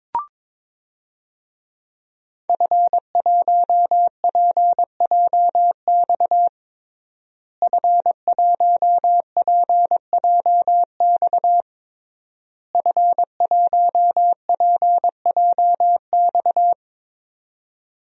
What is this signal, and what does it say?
Morse code, heard as a keyed tone, about 22 wpm, 700 Hz. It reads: F1PJX F1PJX F1PJX
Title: DXCC-100 Call Signs - 5 Characters - Repeated 3x - 22wpm